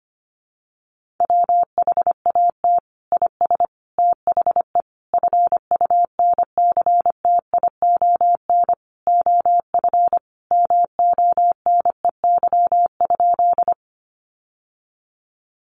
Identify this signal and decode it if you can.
Morse code, heard as a keyed tone, about 25 wpm, 700 Hz. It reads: WHAT IS THE FUNCTION OF MONEY?
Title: Sentences from Top 1100 Words 25wpm